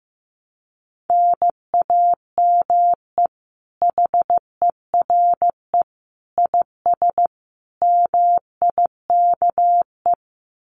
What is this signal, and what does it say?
Morse code, heard as a keyed tone, about 15 wpm, 700 Hz. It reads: NAME HERE IS MIKE
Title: Common QSO Phrases 15wpm